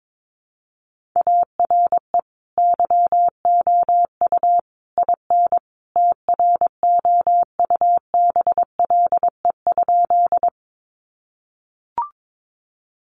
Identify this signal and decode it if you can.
Morse code, heard as a keyed tone, about 22 wpm, 700 Hz. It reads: ARE YOU IN TROUBLE?
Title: Sentences from Top 1200 Words 22wpm